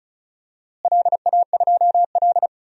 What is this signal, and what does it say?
Morse code, heard as a keyed tone, about 36 wpm, 700 Hz. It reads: LA2L
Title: DXCC-100 Call Signs - 4 Characters - Repeated 2x - 35wpm